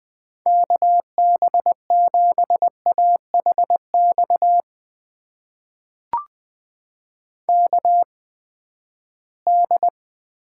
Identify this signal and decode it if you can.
Morse code, heard as a keyed tone, about 20 wpm, spaced as for 14 wpm, 700 Hz. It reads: KB7AHX KD
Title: USA Call Signs - 6 Characters - 8x Character Spacing - 20wpm